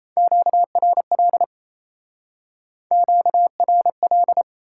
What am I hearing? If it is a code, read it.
Morse code, heard as a keyed tone, about 37 wpm, 700 Hz. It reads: QRL QRL